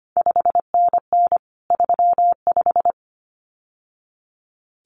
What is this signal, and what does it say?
Morse code, heard as a keyed tone, about 25 wpm, 700 Hz. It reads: 5NN 35
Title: CQ WW DX Contest Exchange Encore - 25wpm